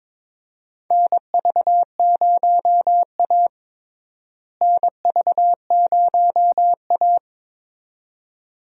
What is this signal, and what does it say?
Morse code, heard as a keyed tone, about 22 wpm, 700 Hz. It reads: NV0A NV0A
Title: USA Call Signs - 4 Characters - Repeated 2x - 22wpm